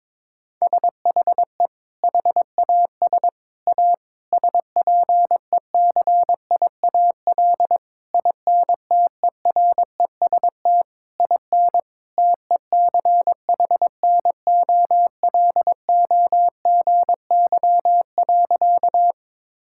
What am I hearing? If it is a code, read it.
Morse code, heard as a keyed tone, about 22 wpm, 700 Hz. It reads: SHE HAS A SPECIAL INTEREST IN TECHNOLOGY.
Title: Sentences from Top 1000 Words - Review Part 5 - 22wpm